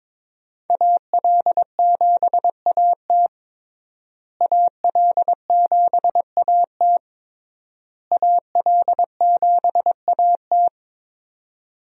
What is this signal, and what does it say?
Morse code, heard as a keyed tone, about 22 wpm, 700 Hz. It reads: AL7AT AL7AT AL7AT
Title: USA Call Signs - 5 Characters - Repeated 3x - 22wpm